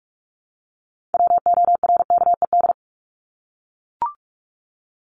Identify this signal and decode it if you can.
Morse code, heard as a keyed tone, about 44 wpm, 700 Hz. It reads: WORKED
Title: ED Suffix - Top 100 Words with Base Words 45wpm